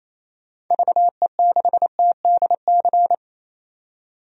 Morse code, heard as a keyed tone, about 28 wpm, 700 Hz. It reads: VE6TDC
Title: DXCC-100 Call Signs - 6 Characters - 28wpm